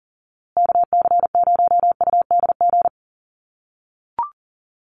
Morse code, heard as a keyed tone, about 40 wpm, 700 Hz. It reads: KC0UDG
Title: USA Call Signs Repeated 3x 40wpm